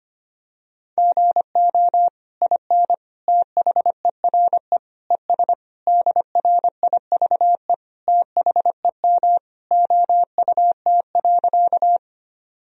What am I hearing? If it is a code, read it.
Morse code, heard as a keyed tone, about 25 wpm, 700 Hz. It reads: GO IN THERE ES DRIVE THEM OUT.
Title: Sentences from Top 500 Words and Abbreviations 25wpm